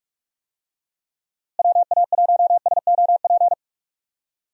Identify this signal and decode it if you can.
Morse code, heard as a keyed tone, about 48 wpm, 700 Hz. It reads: WA1SOP
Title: USA Call Signs - 6 Characters - Repeated 2x - 45wpm